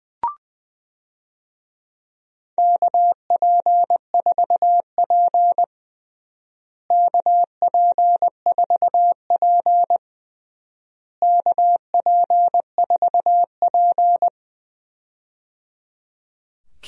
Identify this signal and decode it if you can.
Morse code, heard as a keyed tone, about 21 wpm, 700 Hz. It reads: KP4P KP4P KP4P
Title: DXCC-100 Call Signs - 4 Characters - Repeated 3x - 20wpm